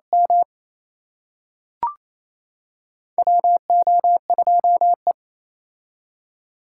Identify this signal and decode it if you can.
Morse code, heard as a keyed tone, about 28 wpm, 700 Hz. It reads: M WO2E
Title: K1USN SST QSO Practice - 28wpm